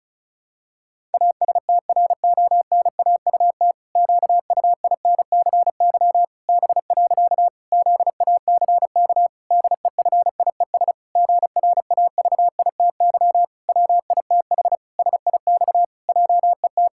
Morse code, heard as a keyed tone, about 36 wpm, 700 Hz. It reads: ASTRONAUT QUINCY B. ZACK DEFIES GRAVITY WITH SIX JET
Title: Pangram Challenge v2 - 35wpm